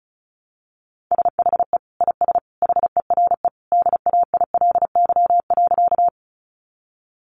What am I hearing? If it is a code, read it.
Morse code, heard as a keyed tone, about 34 wpm, 700 Hz. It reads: SHE IS HERE DAILY.
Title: Sentences from Top 1100 Words 35wpm